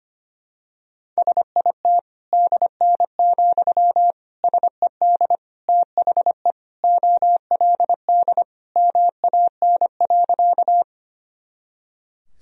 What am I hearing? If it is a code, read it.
Morse code, heard as a keyed tone, about 25 wpm, 700 Hz. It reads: SIT DN, SED THE OLD MAN.
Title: Sentences from Top 500 Words and Abbreviations 25wpm